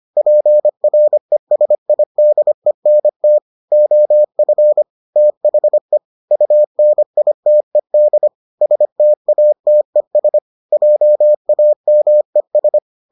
Morse code, heard as a keyed tone, about 25 wpm, 600 Hz. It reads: PRESIDENT OF THE UNITED STATES JAMES